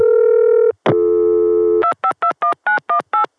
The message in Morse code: none